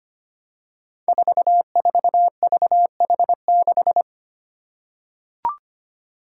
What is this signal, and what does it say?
Morse code, heard as a keyed tone, about 25 wpm, 700 Hz. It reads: 44VH6
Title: Five Challenging Letter-Number Combinations 25wpm